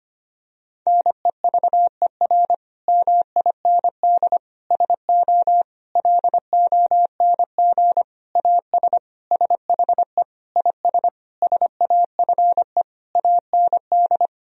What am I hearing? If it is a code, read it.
Morse code, heard as a keyed tone, about 25 wpm, 700 Hz. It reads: NEVER MIND SO LONG AS SHE IS SAFE AND
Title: Sentences from Top 1000 Words - Review Part 1 - 25wpm